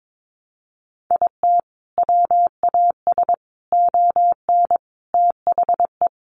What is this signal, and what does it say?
Morse code, heard as a keyed tone, about 22 wpm, 700 Hz. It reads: IT WAS ON THE
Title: Sentences from Top 1200 Words 22wpm